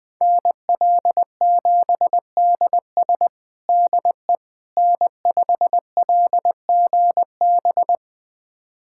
Morse code, heard as a keyed tone, about 21 wpm, 700 Hz. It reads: NL7DS DE N5LGB
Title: USA Callsign DE Callsign - 5 Characters - 20wpm